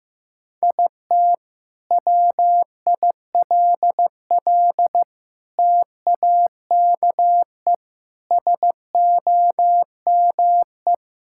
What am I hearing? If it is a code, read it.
Morse code, heard as a keyed tone, about 15 wpm, 700 Hz. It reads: IT WILL TAKE SOME
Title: Sentences from Top 1000 Words - Review Part 5 - 15wpm